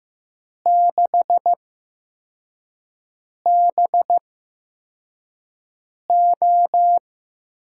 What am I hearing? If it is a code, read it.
Morse code, heard as a keyed tone, about 15 wpm, spaced as for 4 wpm, 700 Hz. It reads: 6BO
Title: USA Call Signs - 8x Character Spacing - 15wpm